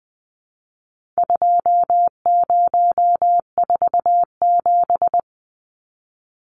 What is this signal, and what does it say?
Morse code, heard as a keyed tone, about 20 wpm, 700 Hz. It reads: 2047
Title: Numbers - 4-Digit 20wpm